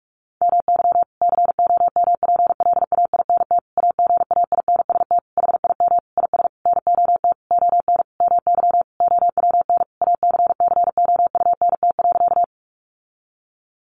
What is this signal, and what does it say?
Morse code, heard as a keyed tone, about 44 wpm, 700 Hz. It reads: MY COMPLAINT AGAINST HIM IS NOT ON MY OWN ACCOUNT.
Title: Sentences from Top 1400 Words 45wpm